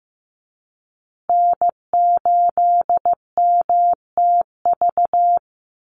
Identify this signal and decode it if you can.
Morse code, heard as a keyed tone, about 15 wpm, 700 Hz. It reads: N8MTV
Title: Call Signs Repeated 3x - 15wpm